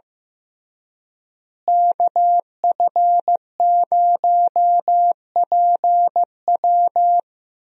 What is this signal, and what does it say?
Morse code, heard as a keyed tone, about 15 wpm, 700 Hz. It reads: KF0PW
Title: USA Call Signs - 5 Characters - Repeated 3x - 15wpm